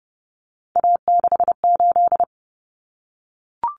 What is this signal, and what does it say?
Morse code, heard as a keyed tone, about 30 wpm, 700 Hz. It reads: A68
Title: Common Cut Numbers - 3-Digit 30wpm